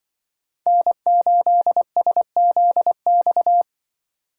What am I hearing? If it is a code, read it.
Morse code, heard as a keyed tone, about 22 wpm, 700 Hz. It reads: N8SZX
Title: Speed Racing - USA 1x3 Call Signs 20wpm